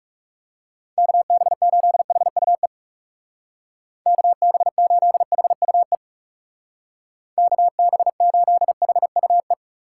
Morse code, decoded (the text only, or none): KB8HUE KB8HUE KB8HUE